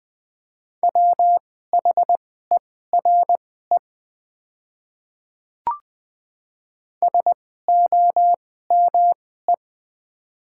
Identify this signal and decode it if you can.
Morse code, heard as a keyed tone, about 21 wpm, spaced as for 15 wpm, 700 Hz. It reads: WHERE SOME
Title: 2x Character Spacing - Top 200 Words - 20wpm